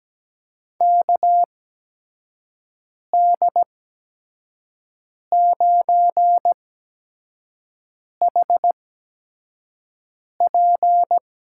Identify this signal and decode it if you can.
Morse code, heard as a keyed tone, about 17 wpm, spaced as for 5 wpm, 700 Hz. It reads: KD9HP
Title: USA Call Signs - 8x Character Spacing - 17wpm